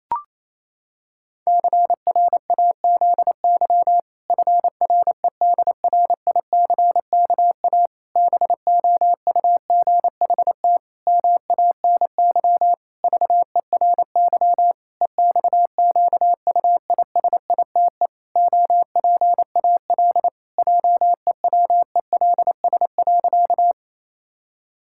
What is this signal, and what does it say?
Morse code, heard as a keyed tone, about 28 wpm, 700 Hz. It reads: CRAZY FREDRICKA BOUGHT MANY VERY EXQUISITE OPAL JEWELS.
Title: Pangram Challenge v2 - 28wpm